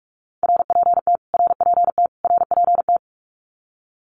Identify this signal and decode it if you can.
Morse code, heard as a keyed tone, about 44 wpm, 700 Hz. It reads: RPT RPT RPT
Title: Abbreviations Repeated 3x - 45wpm